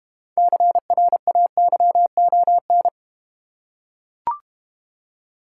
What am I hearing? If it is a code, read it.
Morse code, heard as a keyed tone, about 32 wpm, 690 Hz. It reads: CRAYON